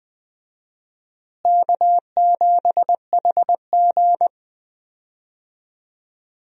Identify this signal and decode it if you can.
Morse code, heard as a keyed tone, about 20 wpm, 700 Hz. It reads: K7HG
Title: USA Call Signs - 4 Characters - Repeated 2x - 20wpm